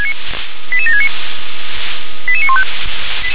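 Five-tone selective call (ZVEI), heard as quick runs of five tones, tones of about 70 ms. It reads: 88769 88915
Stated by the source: ZVEI five-tone selective call received by SDR.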